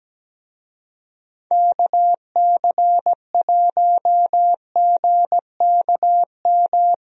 Morse code, heard as a keyed tone, about 17 wpm, 700 Hz. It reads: KC1GKM
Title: USA Call Signs - 6 Characters - Repeated 2x - 17wpm